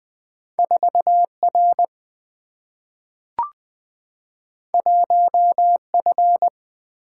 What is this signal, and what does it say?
Morse code, heard as a keyed tone, about 20 wpm, 700 Hz. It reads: 4R 1F
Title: Single-Letter Single-Number 20wpm